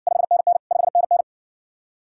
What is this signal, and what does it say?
Morse code, heard as a keyed tone, about 58 wpm, 700 Hz. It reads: 5NN 5NN